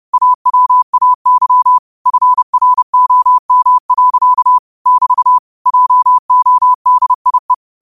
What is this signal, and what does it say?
Morse code, heard as a keyed tone, about 30 wpm, 1000 Hz. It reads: AWAY FROM. <BT> JODIE
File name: qotdfiles / 30wpm